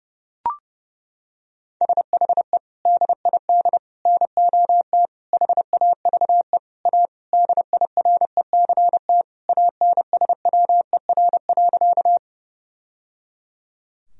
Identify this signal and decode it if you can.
Morse code, heard as a keyed tone, about 30 wpm, 700 Hz. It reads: SHE DID NOT HAVE A DIRECT ANSWER.
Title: Sentences from Top 1000 Words - Review Part 2 - 30wpm